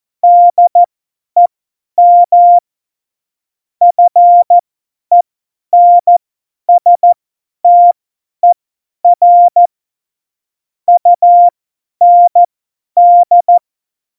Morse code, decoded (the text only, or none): DEM FENSTER UND